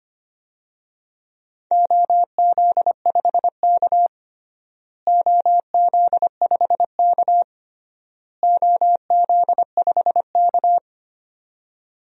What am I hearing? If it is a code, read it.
Morse code, heard as a keyed tone, about 25 wpm, 700 Hz. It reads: OZ5K OZ5K OZ5K
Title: DXCC-100 Call Signs - 4 Characters - Repeated 3x - 25wpm